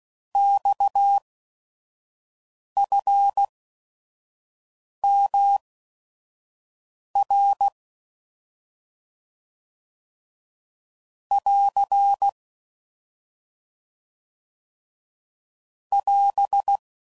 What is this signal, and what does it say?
Morse code, heard as a keyed tone, about 16 wpm, spaced as for 5 wpm, 800 Hz. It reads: XFMR <AR> <AS>
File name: Mixed / Abreviations / Abreviations